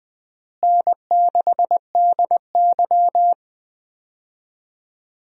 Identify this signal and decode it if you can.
Morse code, heard as a keyed tone, about 20 wpm, 700 Hz. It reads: N6DY
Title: USA Call Signs - 4 Characters - Repeated 3x - 20wpm